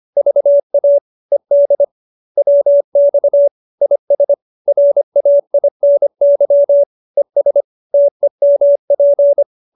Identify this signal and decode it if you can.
Morse code, heard as a keyed tone, about 25 wpm, 570 Hz. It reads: VA ED WX IS RAINY ES TEMP